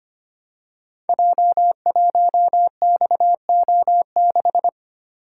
Morse code, heard as a keyed tone, about 25 wpm, 700 Hz. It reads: J1XO6
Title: Code Groups - 5 Letter:Numbers - 25wpm